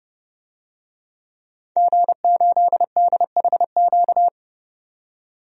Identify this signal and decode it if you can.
Morse code, heard as a keyed tone, about 29 wpm, 700 Hz. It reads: G8DHQ
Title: DXCC-100 Call Signs - 5 Characters - 30wpm